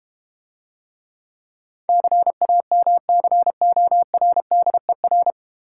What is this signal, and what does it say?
Morse code, heard as a keyed tone, about 30 wpm, 700 Hz. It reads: CAMCORDER
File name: practice / 25wpm / Difficult Words - Speed-Racing with Hints - 25wpm